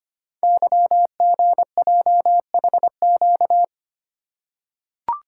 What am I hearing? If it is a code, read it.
Morse code, heard as a keyed tone, about 25 wpm, 700 Hz. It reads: YGJHQ
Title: Code Groups - 5 Letters - 25wpm